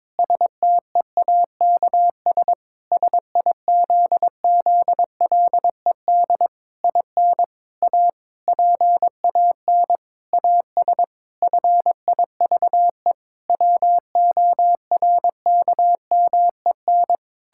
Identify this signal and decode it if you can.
Morse code, heard as a keyed tone, about 22 wpm, 700 Hz. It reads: STEAKS SIZZLED IN A PAN AS FIVE WORKMEN
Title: Pangram Challenge v2 - 22wpm